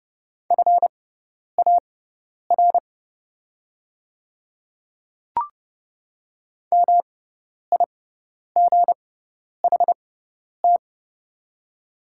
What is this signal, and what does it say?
Morse code, heard as a keyed tone, about 30 wpm, spaced as for 10 wpm, 690 Hz. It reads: FAR MIGHT
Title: 6x Character Spacing - Top 300 Words - 30wpm